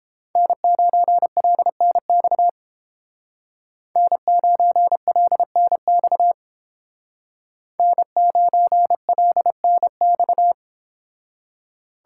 Morse code, heard as a keyed tone, about 31 wpm, 700 Hz. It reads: N9LNX N9LNX N9LNX